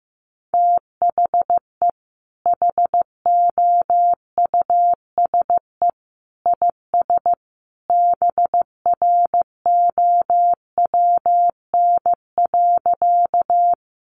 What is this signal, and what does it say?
Morse code, heard as a keyed tone, about 15 wpm, 700 Hz. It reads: THE HOUSE IS BROWN.
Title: Sentences from Top 1200 Words 15wpm